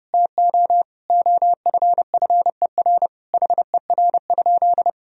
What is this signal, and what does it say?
Morse code, heard as a keyed tone, about 30 wpm, 700 Hz. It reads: TO OFFER HER?
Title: Sentences from Top 1000 Words - Review Part 4 - 30wpm